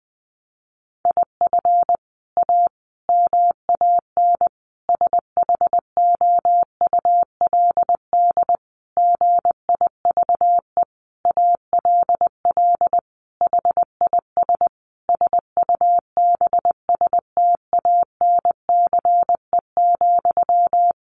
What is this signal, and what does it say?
Morse code, heard as a keyed tone, about 19 wpm, 700 Hz. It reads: IF A MAN SHOULD GIVE ALL HIS SUBSTANCE,
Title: Sentences from Top 1500 Words 20wpm